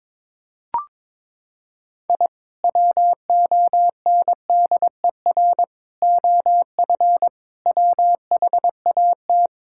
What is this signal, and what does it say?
Morse code, heard as a keyed tone, about 22 wpm, 700 Hz. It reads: I WONDER OF WHAT
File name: practice / 22wpm / Sentences with 2-Word Phrases - 22wpm